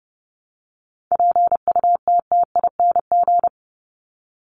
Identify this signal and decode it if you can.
Morse code, heard as a keyed tone, about 30 wpm, 700 Hz. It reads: PUTTING
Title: ING Suffix - Top 100 Words with Base Words 30wpm